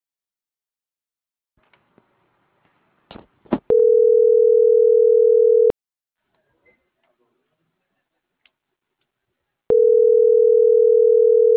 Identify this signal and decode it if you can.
Ringback tone (North America), 440 + 480 Hz.